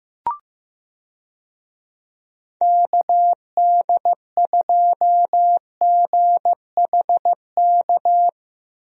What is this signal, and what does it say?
Morse code, heard as a keyed tone, about 15 wpm, 700 Hz. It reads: KD2GHK
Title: USA Call Signs - 6 Characters - 15wpm